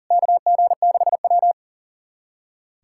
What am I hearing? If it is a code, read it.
Morse code, heard as a keyed tone, about 37 wpm, 700 Hz. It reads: KG6W
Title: USA Callsign DE Callsign - 4 Characters - 40wpm